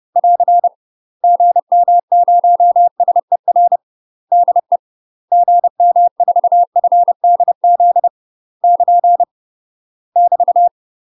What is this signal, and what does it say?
Morse code, heard as a keyed tone, about 30 wpm, 700 Hz. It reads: <AR> GM0SER DE GM4FDZ <KN> <BT>
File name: UK MASTER QSOs TRAINING / 30WPM / UK MASTER QSOs 0047